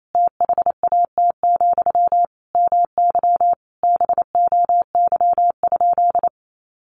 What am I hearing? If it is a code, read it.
Morse code, heard as a keyed tone, about 28 wpm, 700 Hz. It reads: THAT, MY BOY?
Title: Sentences from Top 1100 Words 28wpm